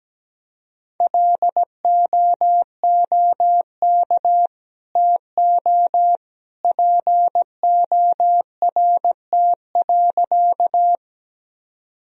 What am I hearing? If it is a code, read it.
Morse code, heard as a keyed tone, about 17 wpm, 700 Hz. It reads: LOOK TO PORT.
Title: Sentences from Top 1000 Words - Review Part 1 - 17wpm